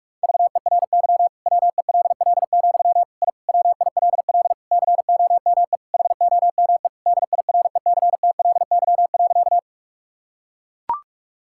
Morse code, heard as a keyed tone, about 41 wpm, 700 Hz. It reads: VERY WELL, I WILL COME HOME DIRECTLY.
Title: Sentences from Top 1000 Words - Review Part 5 - 45wpm